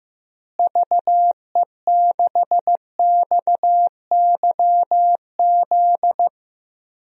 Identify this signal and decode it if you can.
Morse code, heard as a keyed tone, about 15 wpm, 700 Hz. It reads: VE6XYZ
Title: DXCC-100 Call Signs - 6 Characters - Repeated 3x - 15wpm